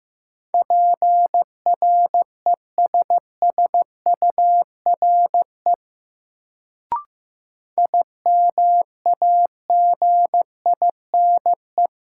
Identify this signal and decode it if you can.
Morse code, heard as a keyed tone, about 15 wpm, 700 Hz. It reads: PRESSURE IMAGINE
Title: Sentences from Top 1400 Words - New Word Drill Alt - 15wpm